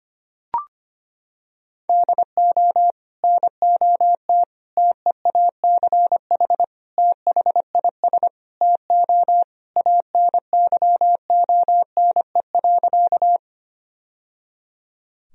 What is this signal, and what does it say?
Morse code, heard as a keyed tone, about 25 wpm, 700 Hz. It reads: DO NOT TEACH THIS TO ANYONE.
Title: Sentences from Top 1000 Words - Review Part 2 - 25wpm